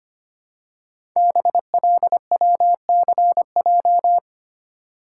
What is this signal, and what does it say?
Morse code, heard as a keyed tone, about 25 wpm, 700 Hz. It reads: BLWCJ
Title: Code Groups - 5 Letters - 25wpm